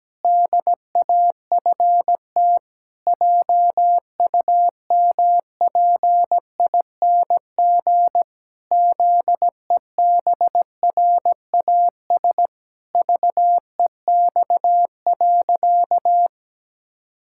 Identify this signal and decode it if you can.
Morse code, heard as a keyed tone, about 17 wpm, 700 Hz. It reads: DAFT JUMPING ZEBRAS VEX.
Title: Pangram Challenge v2 - 17wpm